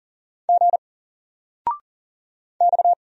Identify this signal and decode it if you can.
Morse code, heard as a keyed tone, about 40 wpm, 700 Hz. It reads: G X
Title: Single Character (Letter:Number:Punctuation:Prosign) 40wpm